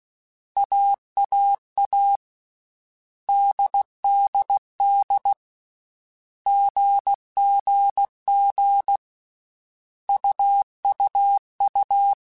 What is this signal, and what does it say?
Morse code, heard as a keyed tone, about 16 wpm, 800 Hz. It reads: AAA DDD GGG UUU
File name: Basic Sets / Basic Letter Groups Collective7 / AWUNDG@13